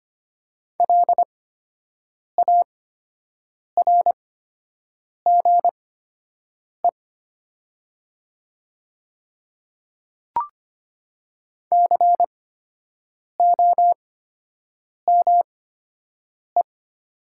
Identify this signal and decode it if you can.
Morse code, heard as a keyed tone, about 25 wpm, spaced as for 7 wpm, 700 Hz. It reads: LARGE COME
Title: 8x Character Spacing - Top 200 Words - 25wpm